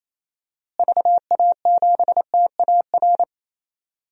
Morse code, heard as a keyed tone, about 28 wpm, 700 Hz. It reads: VA7TAR
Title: DXCC-100 Call Signs - 6 Characters - Repeated 3x - 28wpm